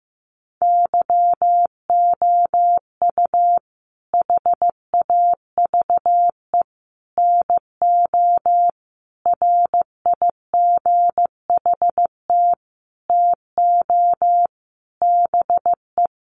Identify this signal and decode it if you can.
Morse code, heard as a keyed tone, about 14 wpm, 700 Hz. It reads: YOU HAVE NO RIGHT TO BE
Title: Sentences from Top 1100 Words 15wpm